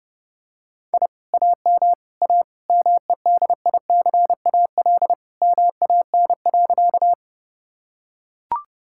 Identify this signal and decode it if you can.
Morse code, heard as a keyed tone, about 30 wpm, 700 Hz. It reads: I AM A MEDICAL MAN.
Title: Sentences from Top 1000 Words - Review Part 5 - 30wpm